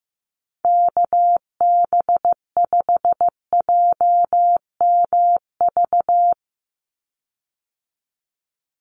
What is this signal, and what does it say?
Morse code, heard as a keyed tone, about 15 wpm, 700 Hz. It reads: KB5JMV
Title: USA Call Signs 15wpm